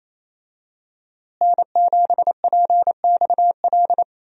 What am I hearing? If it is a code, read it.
Morse code, heard as a keyed tone, about 28 wpm, 700 Hz. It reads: N7PXL
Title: USA Call Signs - 5 Characters - Repeated 3x - 28wpm